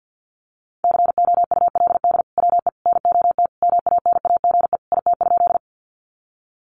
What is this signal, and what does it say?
Morse code, heard as a keyed tone, about 50 wpm, 700 Hz. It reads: COULD WE NOT MANAGE IT?
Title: Sentences from Top 1400 Words 50wpm